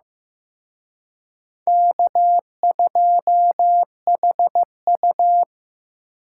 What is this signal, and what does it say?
Morse code, heard as a keyed tone, about 18 wpm, 700 Hz. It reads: K2HU
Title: Speed Racing - USA 1x2 Call Signs 15wpm